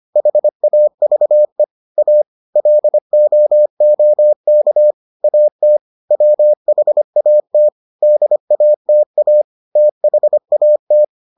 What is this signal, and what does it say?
Morse code, heard as a keyed tone, about 25 wpm, 600 Hz. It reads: HAVE A LOOK AT WHAT DATA THAT